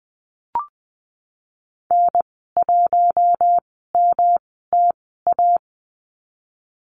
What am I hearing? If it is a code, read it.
Morse code, heard as a keyed tone, about 20 wpm, spaced as for 14 wpm, 700 Hz. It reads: N1MTA
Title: 140 Lesson 23g - Letter f - Call Signs 20wpm